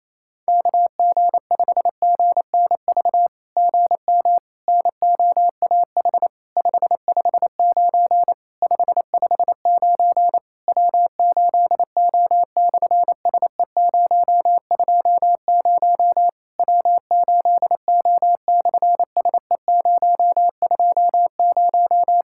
Morse code, heard as a keyed tone, about 27 wpm, 700 Hz. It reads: KG5GNV GM NOAH 559 559 W8O/SE020 W8O/SE020
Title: SOTA QSO Practice 28wpm